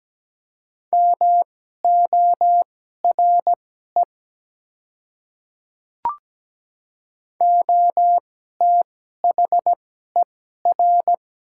Morse code, heard as a keyed tone, about 17 wpm, spaced as for 12 wpm, 700 Hz. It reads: MORE OTHER
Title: 2x Character Spacing - Top 100 Words - 17wpm